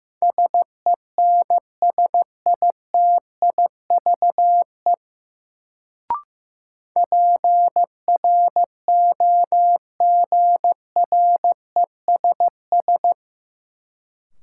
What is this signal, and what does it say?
Morse code, heard as a keyed tone, about 15 wpm, 700 Hz. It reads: SENSITIVE PROGRESS
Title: Sentences from Top 1400 Words - New Word Drill - 15wpm